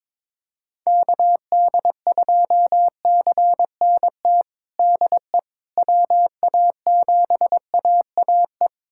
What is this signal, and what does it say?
Morse code, heard as a keyed tone, about 22 wpm, 700 Hz. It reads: KD2CNT DE WA7AAE
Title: USA Callsign DE Callsign - 6 Characters - 22wpm